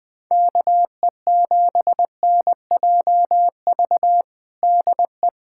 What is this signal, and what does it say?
Morse code, heard as a keyed tone, about 19 wpm, 700 Hz. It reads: KE7NJV DE
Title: USA Callsign DE Callsign 20wpm